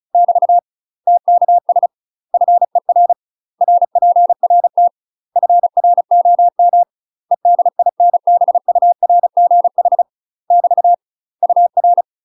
Morse code, heard as a keyed tone, about 35 wpm, 700 Hz. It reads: <BT> TKS FER RPRT FROM EDINBURGH <BT> UR
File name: UK MASTER QSOs TRAINING / 35WPM / UK MASTER QSOs 0042